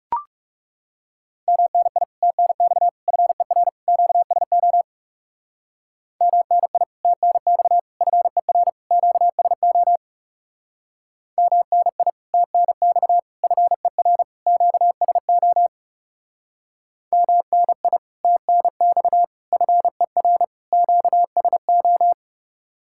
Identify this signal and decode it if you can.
Morse code, heard as a keyed tone, about 36 wpm, 700 Hz. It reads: MNI TNX FER QSO MNI TNX FER QSO MNI TNX FER QSO MNI TNX FER QSO
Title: Speed-Racing Common QSO Phrases - 30wpm